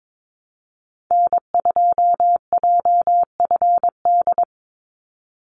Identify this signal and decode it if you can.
Morse code, heard as a keyed tone, about 22 wpm, 700 Hz. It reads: N2JFD
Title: Call Signs Repeated 2x - 22wpm